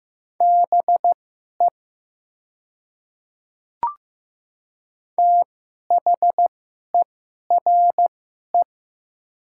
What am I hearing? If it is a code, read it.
Morse code, heard as a keyed tone, about 15 wpm, spaced as for 11 wpm, 700 Hz. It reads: BE THERE
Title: 2x Character Spacing - Top 200 Words - 15wpm